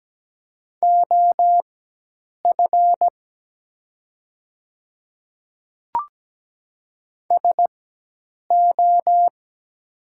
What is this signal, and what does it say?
Morse code, heard as a keyed tone, about 17 wpm, spaced as for 8 wpm, 700 Hz. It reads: OF SO
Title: 4x Character Spacing - Top 300 Words - 17wpm